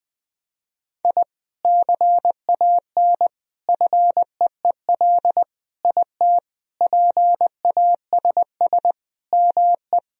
Morse code, heard as a keyed tone, about 20 wpm, 700 Hz. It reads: I CAN FEEL IT PASS ME
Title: Sentences from Top 1000 Words - Review Part 2 - 20wpm